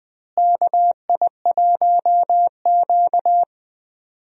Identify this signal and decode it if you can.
Morse code, heard as a keyed tone, about 20 wpm, 700 Hz. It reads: KI1Q
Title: USA Call Signs - 4 Characters - Repeated 2x - 20wpm